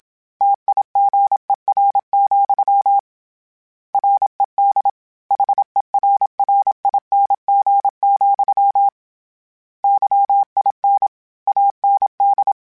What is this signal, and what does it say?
Morse code, heard as a keyed tone, about 27 wpm, 800 Hz. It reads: TIGER, RED HERRING, YIN AND
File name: Misc / Misc-mp3s / Word Pairs @25